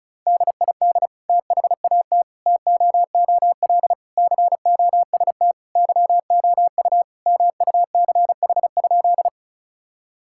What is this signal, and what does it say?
Morse code, heard as a keyed tone, about 34 wpm, 700 Hz. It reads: DID THAT TOOL COST YOU MUCH?
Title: Sentences from Top 1000 Words - Review Part 6 - 35wpm